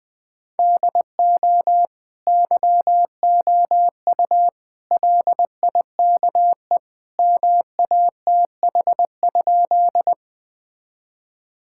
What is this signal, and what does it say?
Morse code, heard as a keyed tone, about 21 wpm, 700 Hz. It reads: DO YOU LIKE MATH?
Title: Sentences from Top 1000 Words - Review Part 6 - 20wpm